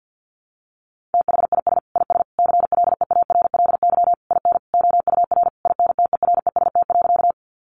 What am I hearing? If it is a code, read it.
Morse code, heard as a keyed tone, about 50 wpm, 700 Hz. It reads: THIS IS CLEARLY IN OUR INTEREST.
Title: Sentences from Top 1300 Words 50wpm